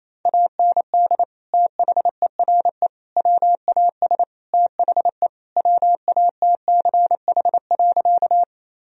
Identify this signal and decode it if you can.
Morse code, heard as a keyed tone, about 28 wpm, 700 Hz. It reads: AND THERE WAS THE WATCH.
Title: Sentences from Top 1000 Words - Review Part 1 - 28wpm